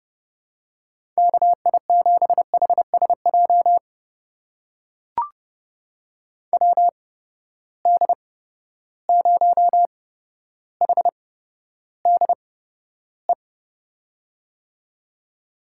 Morse code, heard as a keyed tone, about 30 wpm, spaced as for 13 wpm, 700 Hz. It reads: KI7HSJ WD0HDE
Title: USA Call Signs - 6 Characters - 8x Character Spacing - 30wpm